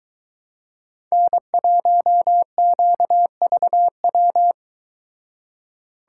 Morse code, heard as a keyed tone, about 22 wpm, 700 Hz. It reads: N1QVW